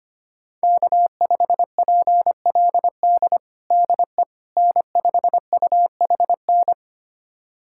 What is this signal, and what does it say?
Morse code, heard as a keyed tone, about 25 wpm, 700 Hz. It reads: K5PLD DE N5UHN
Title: USA Callsign DE Callsign - 5 Characters - 25wpm